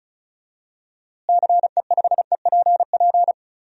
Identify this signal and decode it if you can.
Morse code, heard as a keyed tone, about 36 wpm, 700 Hz. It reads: CE5EPP
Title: DXCC-100 Call Signs - 6 Characters - Repeated 3x - 35wpm